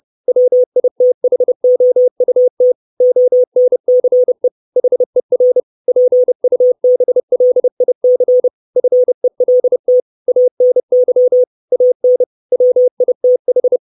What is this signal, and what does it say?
Morse code, heard as a keyed tone, about 30 wpm, 500 Hz. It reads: WITHOUT ONCE HER PUBLIC FELT ANY AN WITH
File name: 632 Words 30WPM